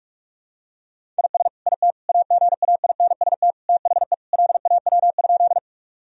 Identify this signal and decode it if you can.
Morse code, heard as a keyed tone, about 41 wpm, 700 Hz. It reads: IS IT AGAINST THE LAW?
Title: Sentences from Top 1000 Words - Review Part 3 - 45wpm